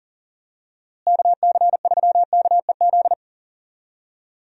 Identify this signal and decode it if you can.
Morse code, heard as a keyed tone, about 40 wpm, 700 Hz. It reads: KC3KEZ
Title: USA Call Signs - 6 Characters - Repeated 3x - 40wpm